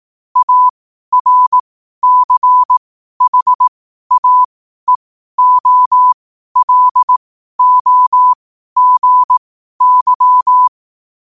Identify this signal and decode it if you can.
Morse code, heard as a keyed tone, about 18 wpm, spaced as for 12 wpm, 1000 Hz. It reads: ARCHAEOLOGY
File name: qotdfiles / 12wpm